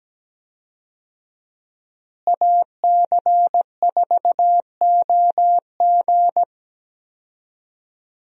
Morse code, heard as a keyed tone, about 17 wpm, 700 Hz. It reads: AC4OG
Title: USA Call Signs - 5 Characters - Repeated 3x - 17wpm